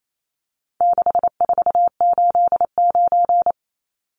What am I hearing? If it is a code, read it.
Morse code, heard as a keyed tone, about 28 wpm, 700 Hz. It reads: 6489
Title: Numbers - 4-Digit 28wpm